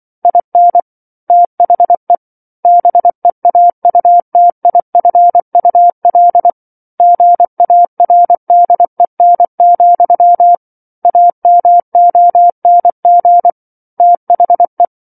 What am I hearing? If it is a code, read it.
Morse code, heard as a keyed tone, about 24 wpm, 700 Hz. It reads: IN THE BEAUTIFUL GARDEN, AMONG THE
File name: 07-Chapter-07 24 WPM